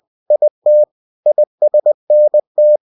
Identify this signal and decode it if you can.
Morse code, heard as a keyed tone, about 20 wpm, 600 Hz. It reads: IT ISNT
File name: cw-books / winnie-the-pooh / winnie-20wpm-page-0027